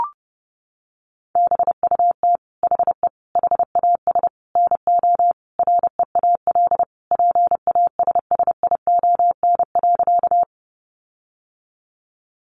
Morse code, heard as a keyed tone, about 30 wpm, 700 Hz. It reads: BUT HE HAS NO REAL PASSION.
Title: Sentences from Top 1300 Words 30wpm